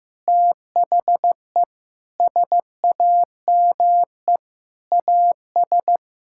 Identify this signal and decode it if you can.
Morse code, heard as a keyed tone, about 15 wpm, 700 Hz. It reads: THE SAME AS
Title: Sentences from Top 1000 Words - Review Part 6 - 15wpm